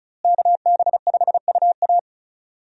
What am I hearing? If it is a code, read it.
Morse code, heard as a keyed tone, about 36 wpm, 700 Hz. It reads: KB5UA
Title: USA Call Signs - 5 Characters - Repeated 3x - 35wpm